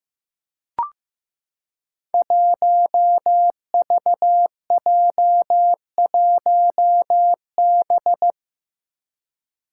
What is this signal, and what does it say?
Morse code, heard as a keyed tone, about 15 wpm, 700 Hz. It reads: 1VJ1B
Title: Five Challenging Letter-Number Combinations 15wpm